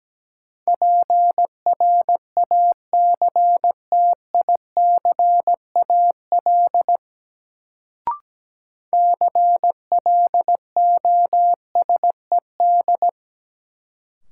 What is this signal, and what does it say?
Morse code, heard as a keyed tone, about 17 wpm, 700 Hz. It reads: PRACTICAL CLOSED
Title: Sentences from Top 1400 Words - New Word Drill Alt - 17wpm